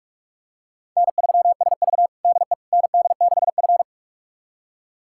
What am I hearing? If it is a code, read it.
Morse code, heard as a keyed tone, about 46 wpm, 700 Hz. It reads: N3SV DE ND6F
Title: USA Callsign DE Callsign - 4 Characters - 45wpm